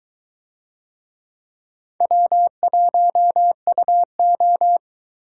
Morse code, heard as a keyed tone, about 22 wpm, 700 Hz. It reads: W1UO